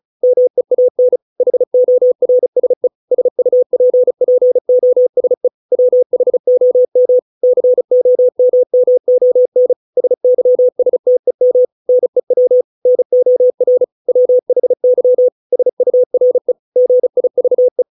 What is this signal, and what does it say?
Morse code, heard as a keyed tone, about 36 wpm, 500 Hz. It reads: MEAN HORSE SUPPOSE WHOM COMMON SYSTEM NEW NOR WHY SURE GIVE